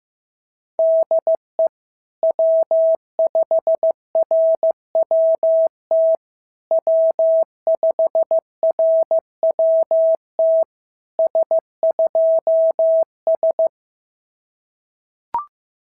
Morse code, heard as a keyed tone, about 14 wpm, 650 Hz. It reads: DE W5RWT W5RWT S2S
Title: SOTA QSO Practice 15wpm